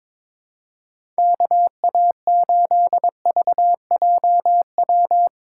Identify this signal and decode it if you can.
Morse code, heard as a keyed tone, about 21 wpm, 700 Hz. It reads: KA8VJW
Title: USA Call Signs - 3x Character Spacing - 22wpm